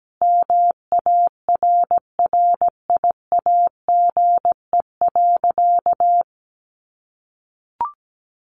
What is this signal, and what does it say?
Morse code, heard as a keyed tone, about 17 wpm, 700 Hz. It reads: MARRIAGE.
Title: Sentences from Top 800 Words 17wpm